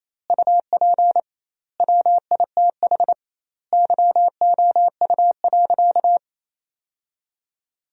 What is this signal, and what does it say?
Morse code, heard as a keyed tone, about 27 wpm, 700 Hz. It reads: UP WITH YOU.
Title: 2x Word-Spacing - Sentences from Top 300 Words - 28wpm